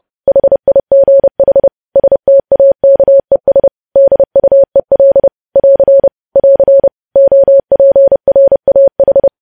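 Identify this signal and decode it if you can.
Morse code, heard as a keyed tone, about 30 wpm, 550 Hz. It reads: HIGH STAKES DUEL <AR> <AR> OPRAH
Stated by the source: Morse news headlines sent at 30 wpm.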